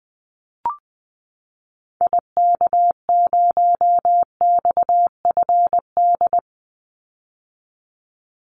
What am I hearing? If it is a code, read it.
Morse code, heard as a keyed tone, about 20 wpm, 700 Hz. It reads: IK0XFD
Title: Call Signs - 20wpm